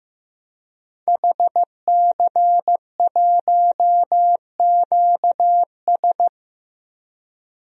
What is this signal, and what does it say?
Morse code, heard as a keyed tone, about 15 wpm, 700 Hz. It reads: HC1QS
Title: DXCC-100 Call Signs - 5 Characters - Repeated 3x - 15wpm